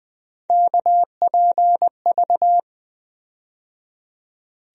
Morse code, heard as a keyed tone, about 20 wpm, 700 Hz. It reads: KPV
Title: USA Call Sign Postfixes - 3 Letters 20wpm